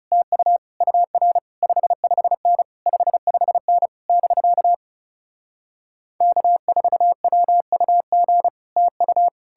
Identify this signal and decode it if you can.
Morse code, heard as a keyed tone, about 40 wpm, 700 Hz. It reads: TU UR 55N 55N <BK> K4WUG TU